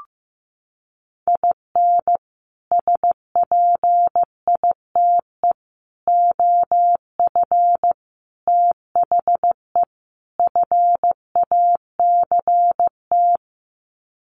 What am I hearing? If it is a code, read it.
Morse code, heard as a keyed tone, about 15 wpm, 700 Hz. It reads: IN SPITE OF THE FACT
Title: Sets of 5 Words Encore2 15wpm